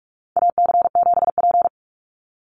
Morse code, heard as a keyed tone, about 44 wpm, 700 Hz. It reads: AC7P
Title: NA SOTA Chaser Callsigns 45wpm